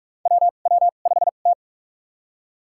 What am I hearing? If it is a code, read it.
Morse code, heard as a keyed tone, about 47 wpm, spaced as for 33 wpm, 700 Hz. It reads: WW5T
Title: USA Call Signs - 4 Characters - 2x Character Spacing - 45wpm